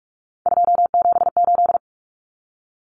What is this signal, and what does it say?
Morse code, heard as a keyed tone, about 44 wpm, 700 Hz. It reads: 278
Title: Common Cut Numbers - 3-Digit 45wpm